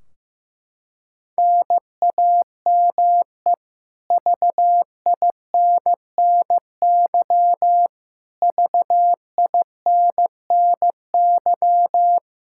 Morse code, heard as a keyed tone, about 15 wpm, 700 Hz. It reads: NAME VINNY VINNY
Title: K1USN SST QSO Practice 15wpm